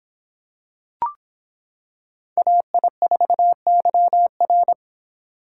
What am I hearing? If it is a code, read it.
Morse code, heard as a keyed tone, about 26 wpm, 700 Hz. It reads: AI4YR